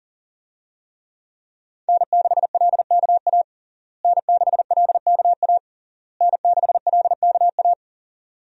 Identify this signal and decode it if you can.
Morse code, heard as a keyed tone, about 40 wpm, 700 Hz. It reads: N6LKA N6LKA N6LKA
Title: USA Call Signs - 5 Characters - Repeated 3x - 40wpm